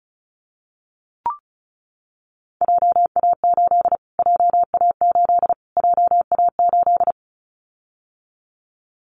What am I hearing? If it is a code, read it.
Morse code, heard as a keyed tone, about 34 wpm, 700 Hz. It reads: JA8 JA8 JA8
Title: Callsign Prefixes for Top 25 DXCC - Repeated 3x - 35wpm